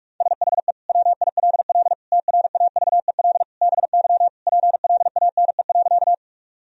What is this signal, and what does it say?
Morse code, heard as a keyed tone, about 41 wpm, 700 Hz. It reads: SHE WILL TRAVEL BY PLANE.
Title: Sentences from Top 1000 Words - Review Part 3 - 45wpm